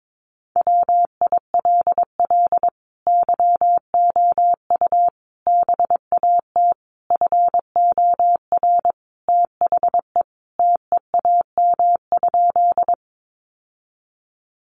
Morse code, heard as a keyed tone, about 22 wpm, 700 Hz. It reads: WILL YOU BAT FOR THE TEAM?
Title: Sentences from Top 1500 Words 22wpm